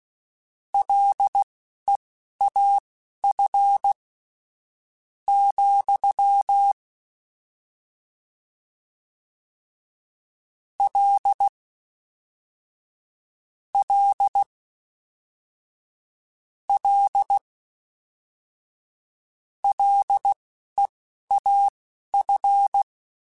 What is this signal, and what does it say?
Morse code, heard as a keyed tone, about 16 wpm, spaced as for 11 wpm, 800 Hz. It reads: LEAF , L L L LEAF